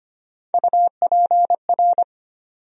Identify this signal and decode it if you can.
Morse code, heard as a keyed tone, about 25 wpm, 700 Hz. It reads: UPR
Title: USA Call Sign Postfixes - 3 Letters 25wpm